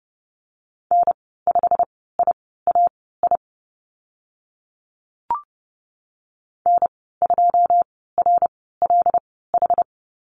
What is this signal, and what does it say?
Morse code, heard as a keyed tone, about 30 wpm, spaced as for 17 wpm, 700 Hz. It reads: N5IAI N2RLH
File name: Lesson 15g - Number 5 - Call Signs 30wpm